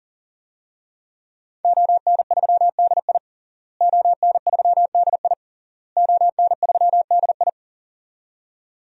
Morse code, heard as a keyed tone, about 40 wpm, 700 Hz. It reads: ON3DI ON3DI ON3DI
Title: DXCC-100 Call Signs - 5 Characters - Repeated 3x - 40wpm